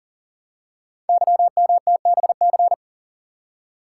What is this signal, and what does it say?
Morse code, heard as a keyed tone, about 42 wpm, 700 Hz. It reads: YMTBC
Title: Code Groups - 5 Letters - 40wpm